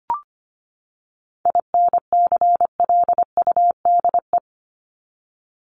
Morse code, heard as a keyed tone, about 25 wpm, 700 Hz. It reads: INCLUDE
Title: ED Suffix - Top 100 Words with Base Words 25wpm